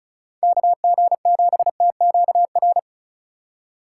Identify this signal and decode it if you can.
Morse code, heard as a keyed tone, about 36 wpm, 700 Hz. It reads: KG7TQR
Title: USA Call Signs - 6 Characters - 2x Character Spacing - 35wpm